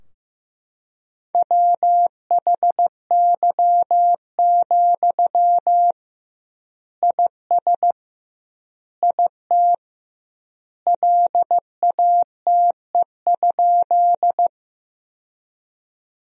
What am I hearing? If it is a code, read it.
Morse code, heard as a keyed tone, about 16 wpm, 700 Hz. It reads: WHY, IS IT LATE?
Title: 2x Word-Spacing - Sentences from Top 300 Words - 15wpm